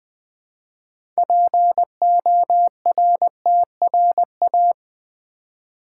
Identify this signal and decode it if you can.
Morse code, heard as a keyed tone, about 20 wpm, 700 Hz. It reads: PORTRA
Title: Build A Word - 2 Letters At A Time - 8-Letter Words - 20wpm